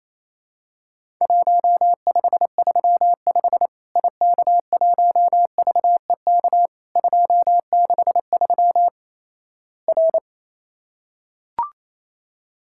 Morse code, heard as a keyed tone, about 27 wpm, 700 Hz. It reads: 1535 IK1VEK 263 R
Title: Worked All Europe DX Contest - QTC Exchange - 28wpm